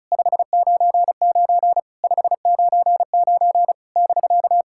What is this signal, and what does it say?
Morse code, heard as a keyed tone, about 36 wpm, 700 Hz. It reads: 599 599 <BK>
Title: SOTA QSO Practice 35wpm